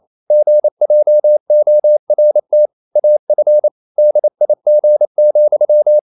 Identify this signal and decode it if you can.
Morse code, heard as a keyed tone, about 28 wpm, 600 Hz. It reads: GJORT AF DIG,
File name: Ferie QRQ 28wpm